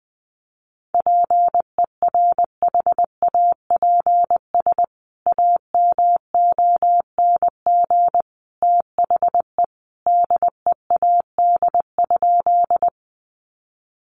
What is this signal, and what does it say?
Morse code, heard as a keyed tone, about 20 wpm, 700 Hz. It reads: PERHAPS AMONG THE DEAD?
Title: Sentences from Top 1500 Words 20wpm